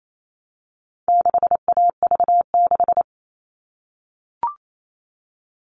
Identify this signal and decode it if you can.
Morse code, heard as a keyed tone, about 28 wpm, 700 Hz. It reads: 6AV6
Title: All Cut Numbers - 4-Digit 28wpm